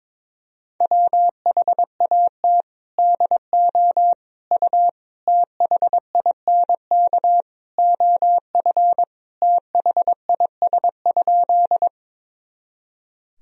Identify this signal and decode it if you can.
Morse code, heard as a keyed tone, about 22 wpm, 700 Hz. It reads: WHAT DO U THINK OF THIS?
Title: Sentences from Top 200 Words and Abbreviations 22wpm